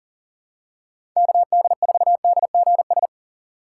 Morse code, heard as a keyed tone, about 37 wpm, 700 Hz. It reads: KD4DGS
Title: USA Call Signs - 6 Characters - 2x Character Spacing - 40wpm